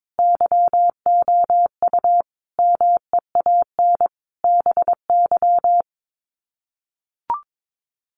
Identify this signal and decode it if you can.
Morse code, heard as a keyed tone, about 22 wpm, 700 Hz. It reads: YOU MEAN BY
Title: Sets of 5 Words 22wpm